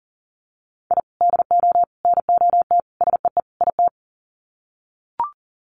Morse code, heard as a keyed tone, about 37 wpm, 700 Hz. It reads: I DO NOT SEE IT
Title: Sets of 5 Words Encore2 40wpm